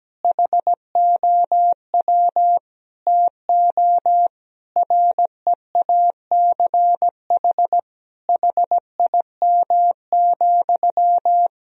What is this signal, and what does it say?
Morse code, heard as a keyed tone, about 17 wpm, 700 Hz. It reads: HOW TO REACH HIM,
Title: Sentences from Top 1000 Words - Review Part 2 - 17wpm